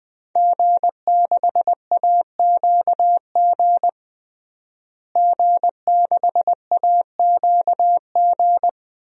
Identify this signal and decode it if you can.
Morse code, heard as a keyed tone, about 20 wpm, 700 Hz. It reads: G6AQG G6AQG
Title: DXCC-100 Call Signs - 5 Characters - Repeated 3x - 20wpm